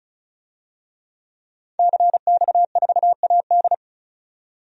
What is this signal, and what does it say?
Morse code, heard as a keyed tone, about 36 wpm, 700 Hz. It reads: CX4AD
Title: DXCC-100 Call Signs - 5 Characters - 35wpm